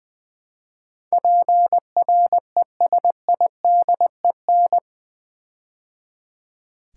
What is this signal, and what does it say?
Morse code, heard as a keyed tone, about 20 wpm, 700 Hz. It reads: PRESIDEN
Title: Build A Word - 4 Letters At A Time - 12-Letter Words - 20wpm